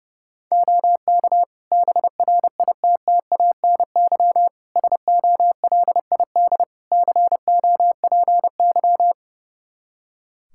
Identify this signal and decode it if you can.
Morse code, heard as a keyed tone, about 30 wpm, 700 Hz. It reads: OK BRITTANY SOLID COPY
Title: K1USN SST QSO Practice 30wpm